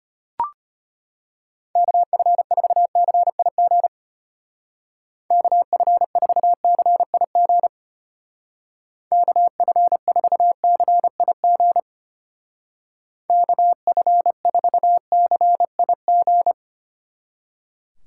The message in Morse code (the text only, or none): KF4CIG KF4CIG KF4CIG KF4CIG